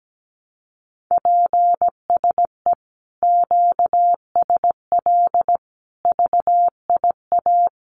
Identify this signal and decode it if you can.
Morse code, heard as a keyed tone, about 17 wpm, 700 Hz. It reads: PSE QSL VIA
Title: Common QSO Phrases Encore 17wpm